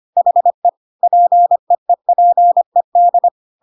Morse code, heard as a keyed tone, about 25 wpm, 700 Hz. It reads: HE PEEPED
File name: Pinocchio / Pinocchio ch1 25 25 ik1hge